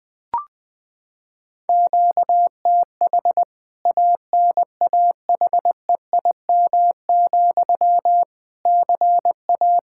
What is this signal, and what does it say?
Morse code, heard as a keyed tone, about 20 wpm, 700 Hz. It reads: QTH ANAHEIM, CA
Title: K1USN SST QSO Practice 20wpm